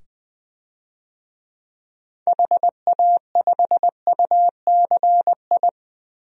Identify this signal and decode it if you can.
Morse code, heard as a keyed tone, about 20 wpm, 700 Hz. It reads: HA5UCI
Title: DXCC-100 Call Signs - 6 Characters - 20wpm